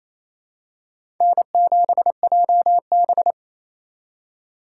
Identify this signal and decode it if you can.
Morse code, heard as a keyed tone, about 28 wpm, 700 Hz. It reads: N7JB
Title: USA Call Signs - 4 Characters - 3x Character Spacing - 28wpm